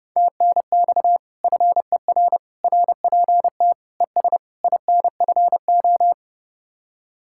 Morse code, heard as a keyed tone, about 38 wpm, 700 Hz. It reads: TNX FER RPT ES INFO